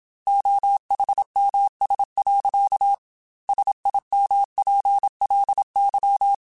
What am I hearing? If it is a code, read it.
Morse code, heard as a keyed tone, about 27 wpm, 800 Hz. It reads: OHMS. SIMPLY
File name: Technical Subjects / Technical Subjects-mp3s / XC @25